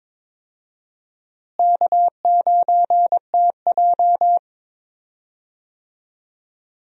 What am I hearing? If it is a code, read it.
Morse code, heard as a keyed tone, about 22 wpm, 700 Hz. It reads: K9TJ
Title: USA Call Signs - 4 Characters - Repeated 2x - 22wpm